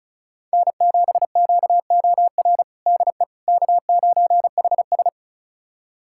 Morse code, heard as a keyed tone, about 34 wpm, 700 Hz. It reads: N7QOR DE K9HS
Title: USA Callsign DE Callsign 35wpm